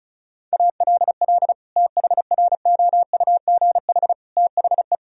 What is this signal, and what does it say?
Morse code, heard as a keyed tone, about 36 wpm, 700 Hz. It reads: ALL THROUGH THE
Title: Sentences from Top 1000 Words - Review Part 2 - 35wpm